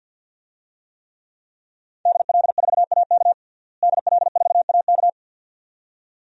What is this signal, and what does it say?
Morse code, heard as a keyed tone, about 51 wpm, 700 Hz. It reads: DL4AK DL4AK
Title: DXCC-100 Call Signs - 5 Characters - Repeated 2x - 50wpm